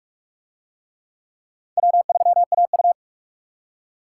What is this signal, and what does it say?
Morse code, heard as a keyed tone, about 44 wpm, 700 Hz. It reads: W3AU